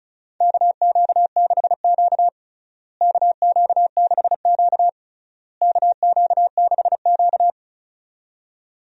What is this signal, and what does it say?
Morse code, heard as a keyed tone, about 36 wpm, 700 Hz. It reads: KQ6Q KQ6Q KQ6Q
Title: USA Call Signs - 4 Characters - Repeated 3x - 35wpm